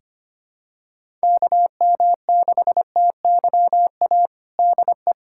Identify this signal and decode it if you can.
Morse code, heard as a keyed tone, about 25 wpm, 700 Hz. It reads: KM6TYA DE
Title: USA Callsign DE Callsign 25wpm